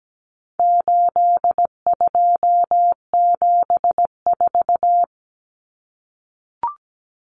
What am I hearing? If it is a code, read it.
Morse code, heard as a keyed tone, about 17 wpm, 700 Hz. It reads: 8274
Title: Numbers - 4-Digit 17wpm